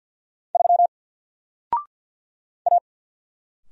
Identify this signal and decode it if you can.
Morse code, heard as a keyed tone, about 48 wpm, 700 Hz. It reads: 3 A
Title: Single Letter-Number Alt 50wpm